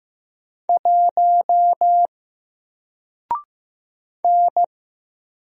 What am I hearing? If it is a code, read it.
Morse code, heard as a keyed tone, about 15 wpm, 700 Hz. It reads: 1 N
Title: Single Letter-Number 15wpm